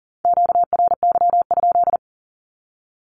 Morse code, heard as a keyed tone, about 41 wpm, 700 Hz. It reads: QRY?
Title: Q-Codes with Spoken Meaning - 40wpm